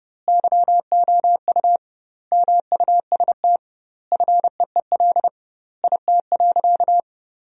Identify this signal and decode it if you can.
Morse code, heard as a keyed tone, about 30 wpm, 700 Hz. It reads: YOU MUST FEEL IT.
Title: 2x Word-Spacing - Sentences from Top 400 Words - 30wpm